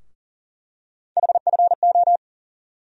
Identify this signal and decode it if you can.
Morse code, heard as a keyed tone, about 40 wpm, 700 Hz. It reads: HFO
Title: USA Call Sign Postfixes - 3 Letters 40wpm